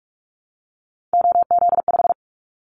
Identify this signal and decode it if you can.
Morse code, heard as a keyed tone, about 46 wpm, 700 Hz. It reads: OZ5
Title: Callsign Prefixes for Top 50 DXCC - 45wpm